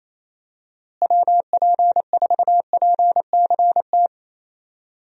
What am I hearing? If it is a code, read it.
Morse code, heard as a keyed tone, about 27 wpm, 700 Hz. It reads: WP4PCT
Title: USA Call Signs - 6 Characters - Repeated 3x - 28wpm